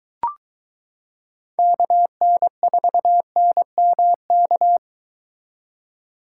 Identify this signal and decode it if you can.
Morse code, heard as a keyed tone, about 23 wpm, 700 Hz. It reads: KN4NMK